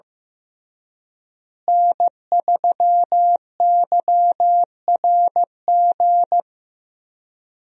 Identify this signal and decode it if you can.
Morse code, heard as a keyed tone, about 16 wpm, 700 Hz. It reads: N3YRG
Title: USA Call Signs - 5 Characters - Repeated 3x - 15wpm